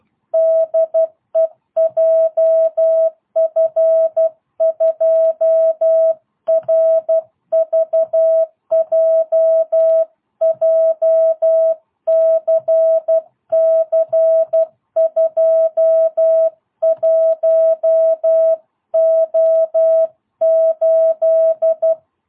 Morse code, heard as a keyed tone, about 12 wpm, 660 Hz. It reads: DE JF2RVJ JCC21O8